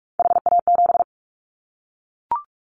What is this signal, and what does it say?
Morse code, heard as a keyed tone, about 44 wpm, 700 Hz. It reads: HA7
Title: Callsign Prefixes for Top 50 DXCC - 45wpm